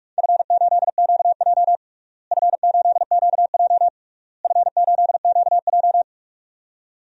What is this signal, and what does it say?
Morse code, heard as a keyed tone, about 46 wpm, 690 Hz. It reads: F8QJ F8QJ F8QJ
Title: DXCC-100 Call Signs - 4 Characters - Repeated 3x - 45wpm